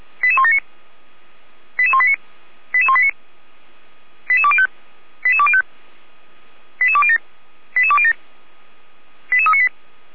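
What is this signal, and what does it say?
Five-tone selective call (ZVEI), heard as quick runs of five tones, tones of about 70 ms. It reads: 88179 88189 88189 88205 88205 88207 88207 88398
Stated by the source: ZVEI five-tone call received by SDR, from the Feuerwehr Orlamünde audio server.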